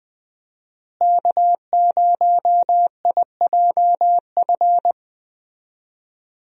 Morse code, heard as a keyed tone, about 21 wpm, 700 Hz. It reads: K0IJF
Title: USA Call Signs - 5 Characters - Repeated 3x - 20wpm